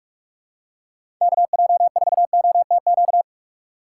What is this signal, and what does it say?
Morse code, heard as a keyed tone, about 46 wpm, 700 Hz. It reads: KJ4OTQ
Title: USA Call Signs - 6 Characters - 3x Character Spacing - 45wpm